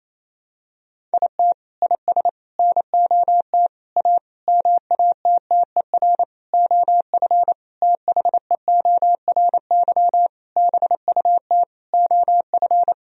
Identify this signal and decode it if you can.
Morse code, heard as a keyed tone, about 28 wpm, 700 Hz. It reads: IT IS NOT A MATTER OF THEORY BUT OF
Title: Sentences from Top 1000 Words - Review Part 4 - 28wpm